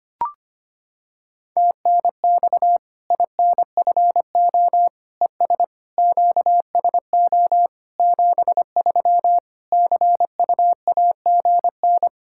Morse code, heard as a keyed tone, about 25 wpm, 700 Hz. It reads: TNX INFO ES QSO 73 CUAGN
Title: Practice QSO - Exchange 4 - Ending - 25wpm